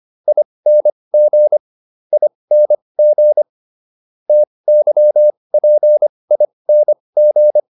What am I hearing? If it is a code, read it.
Morse code, heard as a keyed tone, about 25 wpm, spaced as for 20 wpm, 600 Hz. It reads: ING ING TYPING